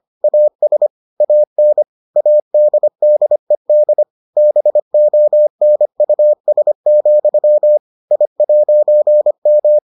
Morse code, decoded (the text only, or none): AS AN ADDED BONUS, I'M